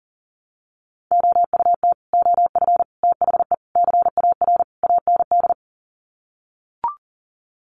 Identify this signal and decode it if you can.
Morse code, heard as a keyed tone, about 40 wpm, 700 Hz. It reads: OUT OF THE CAR AND
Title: Sets of 5 Words Encore 40wpm